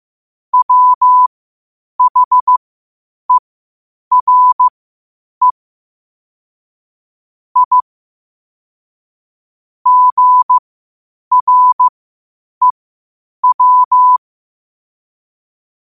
Morse code, heard as a keyed tone, about 15 wpm, spaced as for 8 wpm, 1000 Hz. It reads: WHERE I GREW
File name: qotdfiles / 8wpm